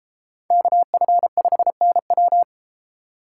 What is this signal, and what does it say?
Morse code, heard as a keyed tone, about 32 wpm, 700 Hz. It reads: KF5NW